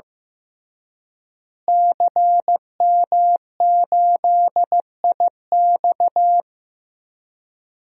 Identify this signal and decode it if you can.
Morse code, heard as a keyed tone, about 16 wpm, 700 Hz. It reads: CM8IX
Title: DXCC-100 Call Signs - 5 Characters - Repeated 2x - 15wpm